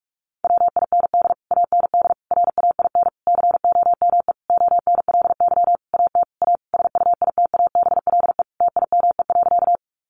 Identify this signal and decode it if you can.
Morse code, heard as a keyed tone, about 44 wpm, 700 Hz. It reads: WIND AND RAIN COME ONLY AT A SUITABLE TIME.
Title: Sentences from Top 1400 Words 45wpm